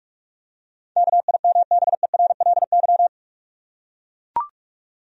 Morse code, heard as a keyed tone, about 41 wpm, 700 Hz. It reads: KIMBERLY
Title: USA Common Names - 45wpm